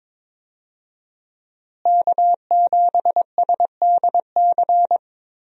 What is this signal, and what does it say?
Morse code, heard as a keyed tone, about 22 wpm, 700 Hz. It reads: K7SDC
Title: USA Call Signs - 5 Characters - Repeated 3x - 22wpm